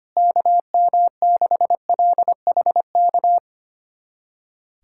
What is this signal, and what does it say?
Morse code, heard as a keyed tone, about 25 wpm, 700 Hz. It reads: KM6LHK
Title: USA Callsign DE Callsign - 6 Characters - 25wpm